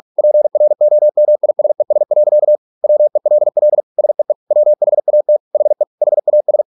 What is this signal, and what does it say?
Morse code, heard as a keyed tone, about 46 wpm, 600 Hz. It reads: PROMISES. WELL SEE WHAT HE HAS